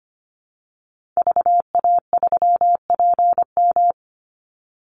Morse code, heard as a keyed tone, about 25 wpm, 700 Hz. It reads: VA3PM
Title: Call Signs - 25wpm